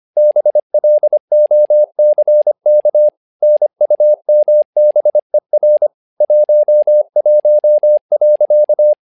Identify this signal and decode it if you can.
Morse code, heard as a keyed tone, about 25 wpm, 600 Hz. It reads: BLOCK NUMBER 11.
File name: learning center / tower trouble / tower trouble 25wpm